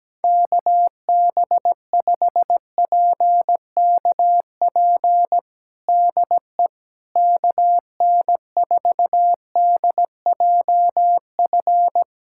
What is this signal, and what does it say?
Morse code, heard as a keyed tone, about 17 wpm, 700 Hz. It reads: KB5PKP DE KN4DJF
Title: USA Callsign DE Callsign - 6 Characters - 17wpm